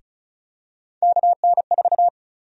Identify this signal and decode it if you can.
Morse code, heard as a keyed tone, about 36 wpm, 700 Hz. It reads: KN4
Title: USA Call Sign Prefixes - Repeated 2x - 35wpm